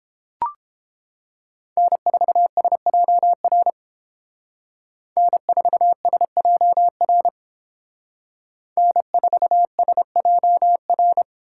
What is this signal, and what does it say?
Morse code, heard as a keyed tone, about 30 wpm, 700 Hz. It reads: N4SJR N4SJR N4SJR